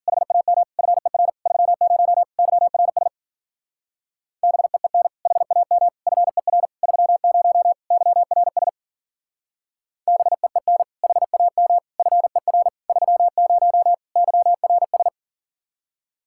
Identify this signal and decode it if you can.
Morse code, heard as a keyed tone, about 48 wpm, 700 Hz. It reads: HAM FER 30 YRS BEEN HAM FER 30 YRS BEEN HAM FER 30 YRS
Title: Speed-Racing Common QSO Phrases - 40wpm